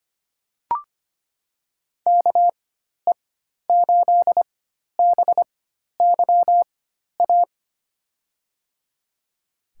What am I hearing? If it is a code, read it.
Morse code, heard as a keyed tone, about 25 wpm, spaced as for 12 wpm, 700 Hz. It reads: KE8BYA
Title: USA Call Signs - 6 Characters - 4x Character Spacing - 25wpm